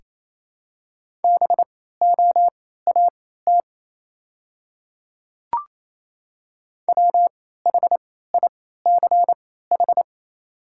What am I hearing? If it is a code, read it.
Morse code, heard as a keyed tone, about 28 wpm, spaced as for 16 wpm, 700 Hz. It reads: BOAT WHICH
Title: 3x Character Spacing - Top 500 Words - 28wpm